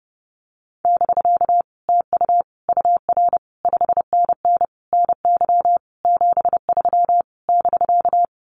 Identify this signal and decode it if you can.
Morse code, heard as a keyed tone, about 31 wpm, 700 Hz. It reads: <BK> TU UR 5NN NY 73 <BK>
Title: Common QSO Phrases Encore 30wpm